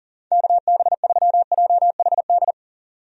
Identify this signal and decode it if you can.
Morse code, heard as a keyed tone, about 40 wpm, 700 Hz. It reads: KB3JHD
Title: USA Call Signs - 6 Characters - 40wpm